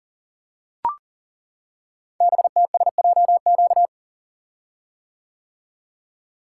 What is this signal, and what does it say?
Morse code, heard as a keyed tone, about 40 wpm, 700 Hz. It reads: BTSJQ
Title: Code Groups - 5 Letters - 40wpm